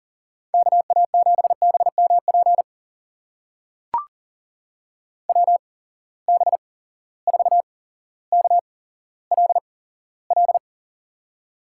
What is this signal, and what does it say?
Morse code, heard as a keyed tone, about 37 wpm, spaced as for 17 wpm, 700 Hz. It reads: KA7BMP WB4KLL
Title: USA Call Signs - 6 Characters - 8x Character Spacing - 40wpm